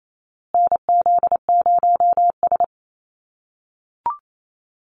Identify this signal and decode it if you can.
Morse code, heard as a keyed tone, about 28 wpm, 700 Hz. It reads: NZ0S
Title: USA Call Signs Repeated 3x 28wpm